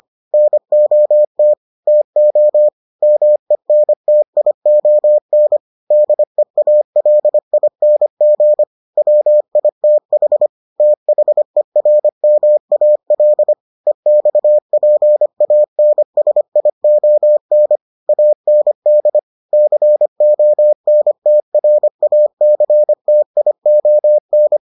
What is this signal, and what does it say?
Morse code, heard as a keyed tone, about 25 wpm, 600 Hz. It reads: NOT TO MENTION DEALING WITH THERMAL EXPANSION AND CONTRACTION